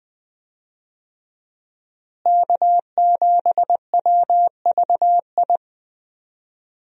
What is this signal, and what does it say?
Morse code, heard as a keyed tone, about 20 wpm, 700 Hz. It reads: K7WVI
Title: USA Call Signs - 5 Characters - Repeated 2x - 20wpm